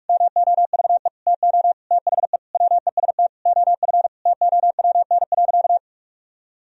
Morse code, heard as a keyed tone, about 44 wpm, 700 Hz. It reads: MOVE TO THE WEST OF TOWN.
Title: Sentences from Top 1000 Words - Review Part 2 - 45wpm